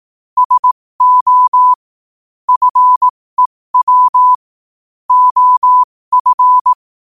Morse code, heard as a keyed tone, about 18 wpm, spaced as for 15 wpm, 1000 Hz. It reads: SO FEW OF
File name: qotdfiles / 15wpm